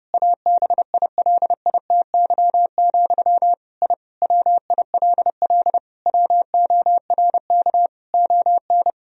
Morse code, heard as a keyed tone, about 32 wpm, 700 Hz. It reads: ABILITY, I WILL WORK ON
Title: Sentences from Top 1000 Words - Review Part 3 - 30wpm